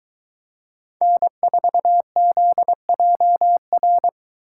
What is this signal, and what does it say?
Morse code, heard as a keyed tone, about 23 wpm, 700 Hz. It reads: N4ZJR